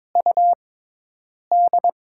Morse code, heard as a keyed tone, about 22 wpm, 690 Hz. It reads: U D
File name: Top 20 QSO Elements - 6x Character Spacing - 22wpm